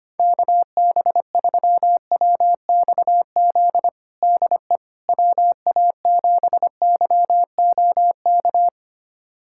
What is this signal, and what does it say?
Morse code, heard as a keyed tone, about 25 wpm, 700 Hz. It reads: KB3WXZ DE WA7YOK
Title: USA Callsign DE Callsign - 6 Characters - 25wpm